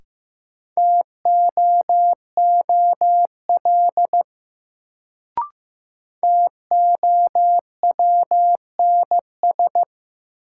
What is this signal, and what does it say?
Morse code, heard as a keyed tone, about 15 wpm, 700 Hz. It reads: TOOL TOWNS
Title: Sentences from Top 1000 Words - Review Part 6 - New Word Drill - 15wpm